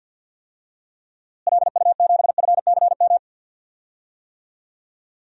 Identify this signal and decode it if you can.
Morse code, heard as a keyed tone, about 50 wpm, 700 Hz. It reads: LU7FCM
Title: DXCC-100 Call Signs - 6 Characters - Repeated 2x - 50wpm